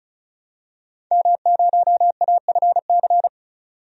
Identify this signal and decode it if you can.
Morse code, heard as a keyed tone, about 36 wpm, 700 Hz. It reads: M0AFC
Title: DXCC-100 Call Signs - 5 Characters - 35wpm